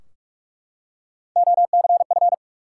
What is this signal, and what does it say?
Morse code, heard as a keyed tone, about 44 wpm, 700 Hz. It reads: OCF
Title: Top 180 QSO Elements 45wpm